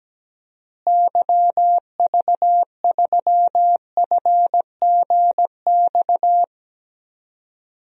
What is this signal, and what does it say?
Morse code, heard as a keyed tone, about 17 wpm, 700 Hz. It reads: YV3FGX
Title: DXCC-100 Call Signs - 6 Characters - Repeated 2x - 17wpm